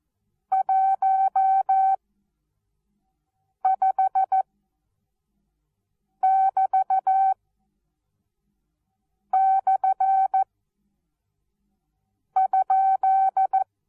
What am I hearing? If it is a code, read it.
Morse code, heard as a keyed tone, about 14 wpm, spaced as for 4 wpm, 770 Hz. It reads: 15<BT>/?
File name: Onlinelehrgang / morsen / DJ4UF Morsen19